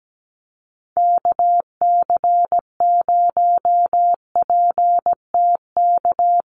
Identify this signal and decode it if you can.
Morse code, heard as a keyed tone, about 17 wpm, 700 Hz. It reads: KC0PTK
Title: USA Call Signs Repeated 2x 17wpm